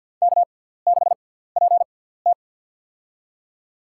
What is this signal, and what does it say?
Morse code, heard as a keyed tone, about 48 wpm, spaced as for 17 wpm, 700 Hz. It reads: K6PT
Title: USA Call Signs - 4 Characters - 6x Character Spacing - 50wpm